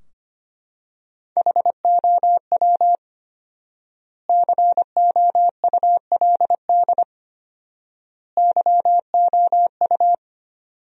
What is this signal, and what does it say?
Morse code, heard as a keyed tone, about 25 wpm, 700 Hz. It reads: HOW COULD YOU
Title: 4x Word-Spacing - Sentences from Top 300 Words - 25wpm